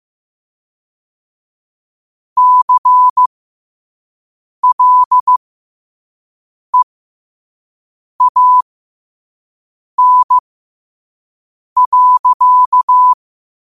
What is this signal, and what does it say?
Morse code, heard as a keyed tone, about 15 wpm, spaced as for 5 wpm, 1000 Hz. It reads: CLEAN.
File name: qotdfiles / 5wpm